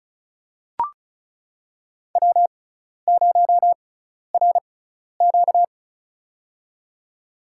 Morse code, heard as a keyed tone, about 34 wpm, spaced as for 12 wpm, 700 Hz. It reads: W0RQ
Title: USA Call Signs - 4 Characters - 6x Character Spacing - 35wpm